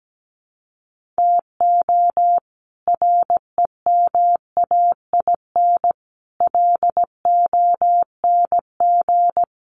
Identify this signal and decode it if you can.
Morse code, heard as a keyed tone, about 17 wpm, 700 Hz. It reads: TO REMAIN LONG
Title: Sentences from Top 1500 Words 17wpm